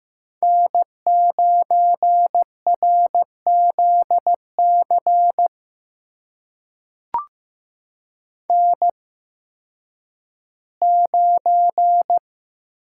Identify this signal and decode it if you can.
Morse code, heard as a keyed tone, about 15 wpm, spaced as for 10 wpm, 700 Hz. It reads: N9RZC N9
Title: USA Call Signs - 8x Character Spacing - 15wpm